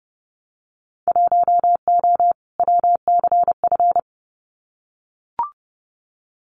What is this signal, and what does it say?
Morse code, heard as a keyed tone, about 30 wpm, 700 Hz. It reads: 1O WCF
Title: Winter Field Day Exchanges 30wpm